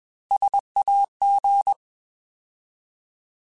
Morse code, heard as a keyed tone, about 21 wpm, 800 Hz. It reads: SAG
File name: Alphabetical / Alphabetical mp3s / Ss @20